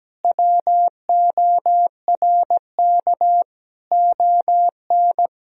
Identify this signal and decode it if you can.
Morse code, heard as a keyed tone, about 17 wpm, 700 Hz. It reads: WORK ON
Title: Sentences from Top 1000 Words - Review Part 1 - 17wpm